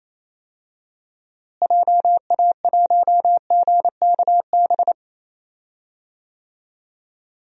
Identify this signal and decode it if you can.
Morse code, heard as a keyed tone, about 27 wpm, 700 Hz. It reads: JA1GKB
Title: DXCC-100 Call Signs - 6 Characters - Repeated 2x - 28wpm